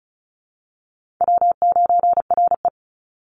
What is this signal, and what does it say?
Morse code, heard as a keyed tone, about 34 wpm, 700 Hz. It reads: W9RE
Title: Easy Call Signs Repeated 2x - 35wpm